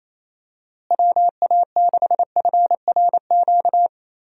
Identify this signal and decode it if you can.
Morse code, heard as a keyed tone, about 28 wpm, 700 Hz. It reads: WA6FRQ
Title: USA Call Signs - 6 Characters - Repeated 2x - 28wpm